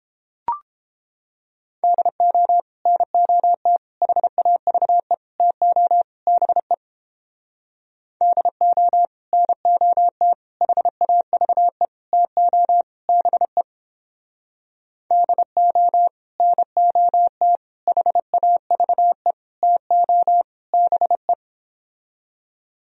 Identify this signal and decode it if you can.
Morse code, heard as a keyed tone, about 29 wpm, 700 Hz. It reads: DO NOT HAVE TO BE DO NOT HAVE TO BE DO NOT HAVE TO BE